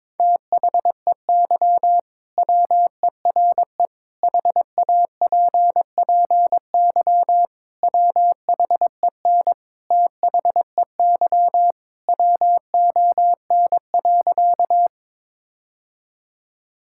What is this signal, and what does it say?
Morse code, heard as a keyed tone, about 22 wpm, 700 Hz. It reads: THEY WERE HAPPY WHEN THEY WON.
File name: practice / 22wpm / Sentences with 2-Word Phrases - 22wpm